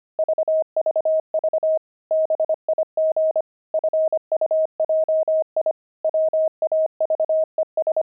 Morse code, heard as a keyed tone, about 25 wpm, 630 Hz. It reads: VVV BIG FUJI WAVES
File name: pangram-fs-25-cs-25-fuji